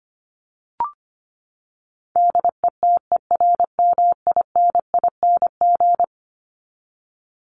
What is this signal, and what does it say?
Morse code, heard as a keyed tone, about 25 wpm, 700 Hz. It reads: DETERMINING
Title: ING Suffix - Top 100 Words with Base Words 25wpm